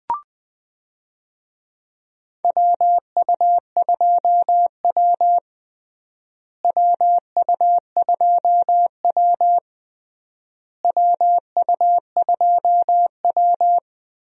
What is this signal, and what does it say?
Morse code, heard as a keyed tone, about 20 wpm, 700 Hz. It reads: WU2W WU2W WU2W
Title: USA Call Signs - 4 Characters - Repeated 3x - 20wpm